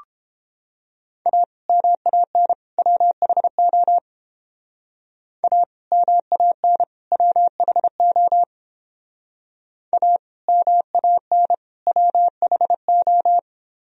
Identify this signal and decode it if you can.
Morse code, heard as a keyed tone, about 28 wpm, 700 Hz. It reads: A MAN WHO A MAN WHO A MAN WHO